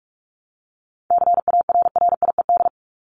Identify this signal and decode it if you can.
Morse code, heard as a keyed tone, about 44 wpm, 700 Hz. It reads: CARRIED
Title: ED Suffix - Top 100 Words with Base Words 45wpm